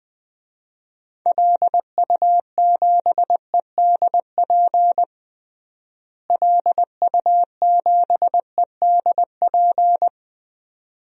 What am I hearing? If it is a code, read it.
Morse code, heard as a keyed tone, about 20 wpm, 700 Hz. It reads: LU7EDP LU7EDP
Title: DXCC-100 Call Signs - 6 Characters - Repeated 2x - 20wpm